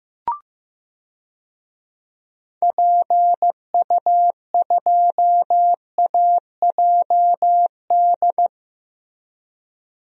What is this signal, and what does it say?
Morse code, heard as a keyed tone, about 15 wpm, 700 Hz. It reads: PU2AJD
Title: DXCC-100 Call Signs - 6 Characters - Repeated 3x - 15wpm